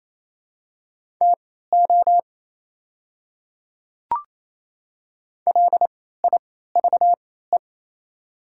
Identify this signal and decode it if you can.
Morse code, heard as a keyed tone, about 28 wpm, spaced as for 16 wpm, 700 Hz. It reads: TO LIVE
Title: 3x Character Spacing - Top 200 Words - 28wpm